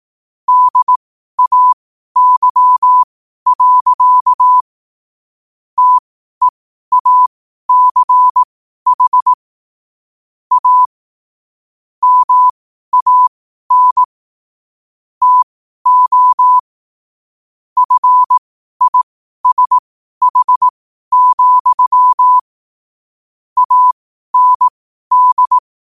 Morse code, heard as a keyed tone, about 18 wpm, spaced as for 12 wpm, 1000 Hz. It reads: DAY. TEACH A MAN TO FISH, AND
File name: qotdfiles / 12wpm